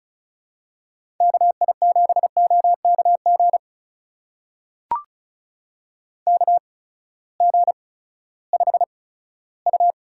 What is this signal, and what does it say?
Morse code, heard as a keyed tone, about 36 wpm, spaced as for 18 wpm, 700 Hz. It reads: KI7OKG KG5U
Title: USA Call Signs - 6 Characters - 8x Character Spacing - 35wpm